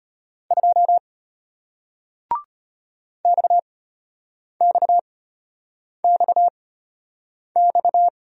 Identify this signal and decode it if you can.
Morse code, heard as a keyed tone, about 34 wpm, spaced as for 8 wpm, 690 Hz. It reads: 2 XXXX